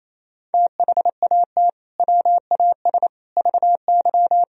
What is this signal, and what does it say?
Morse code, heard as a keyed tone, about 27 wpm, 700 Hz. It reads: THAT WAS VY
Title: Sentences from Top 200 Words and Abbreviations 28wpm